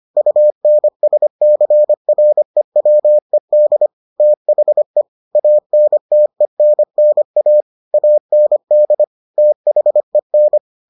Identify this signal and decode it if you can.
Morse code, heard as a keyed tone, about 25 wpm, 600 Hz. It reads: UNSCREWED THE ANTENNA AND THEN